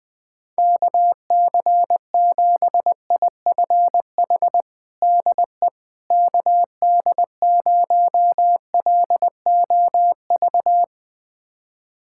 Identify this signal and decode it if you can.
Morse code, heard as a keyed tone, about 20 wpm, 700 Hz. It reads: KC7IFH DE KD0LOV
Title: USA Callsign DE Callsign 20wpm